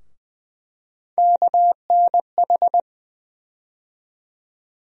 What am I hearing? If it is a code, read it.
Morse code, heard as a keyed tone, about 20 wpm, 700 Hz. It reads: KNH
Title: USA Call Sign Postfixes - 3 Letters 20wpm